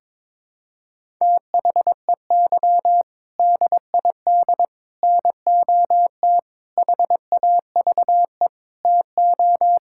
Morse code, heard as a keyed tone, about 22 wpm, 700 Hz. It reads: THEY DID NOT HAVE TO
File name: practice / 22wpm / Sentences with 4-Word Phrases - 22wpm